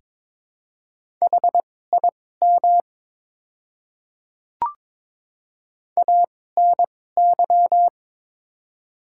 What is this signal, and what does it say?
Morse code, heard as a keyed tone, about 23 wpm, spaced as for 16 wpm, 700 Hz. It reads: HIM ANY
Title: 2x Character Spacing - Top 200 Words - 22wpm